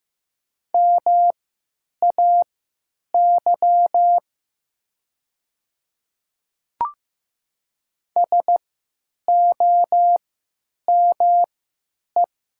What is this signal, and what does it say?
Morse code, heard as a keyed tone, about 15 wpm, spaced as for 9 wpm, 700 Hz. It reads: MAY SOME
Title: 3x Character Spacing - Top 100 Words - 15wpm